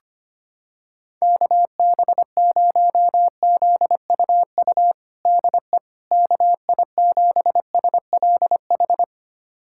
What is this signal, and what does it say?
Morse code, heard as a keyed tone, about 25 wpm, 700 Hz. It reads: KB0ZUU DE KI7SLH
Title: USA Callsign DE Callsign - 6 Characters - 25wpm